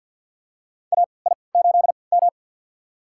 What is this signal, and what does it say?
Morse code, heard as a keyed tone, about 51 wpm, spaced as for 28 wpm, 700 Hz. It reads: AI8M
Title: USA Call Signs - 3x Character Spacing - 50wpm